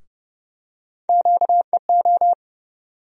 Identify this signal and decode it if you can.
Morse code, heard as a keyed tone, about 30 wpm, 700 Hz. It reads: QEO
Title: USA Call Sign Postfixes - 3 Letters 30wpm